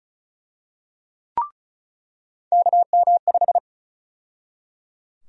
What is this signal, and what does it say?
Morse code, heard as a keyed tone, about 34 wpm, 700 Hz. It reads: KM5
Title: USA Call Sign Prefixes - 35wpm